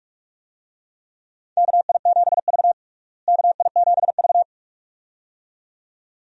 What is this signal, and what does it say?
Morse code, heard as a keyed tone, about 46 wpm, 700 Hz. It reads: KI7V KI7V
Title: USA Call Signs - 4 Characters - Repeated 2x - 45wpm